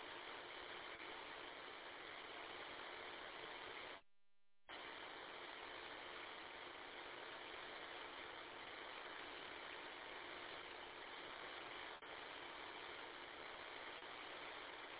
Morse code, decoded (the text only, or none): none